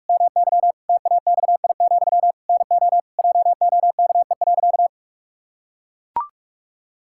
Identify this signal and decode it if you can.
Morse code, heard as a keyed tone, about 41 wpm, 700 Hz. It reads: MY TAXI, NO JOKE.
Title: Pangram Challenge v2 - 45wpm